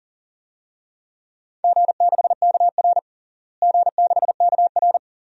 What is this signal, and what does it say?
Morse code, heard as a keyed tone, about 40 wpm, 700 Hz. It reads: G6KR G6KR
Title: DXCC-100 Call Signs - 4 Characters - Repeated 2x - 40wpm